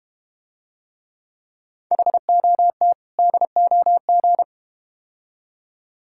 Morse code, heard as a keyed tone, about 31 wpm, 700 Hz. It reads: HOT DOG
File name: Easy Words - Speed-Racing with Hints - 25wpm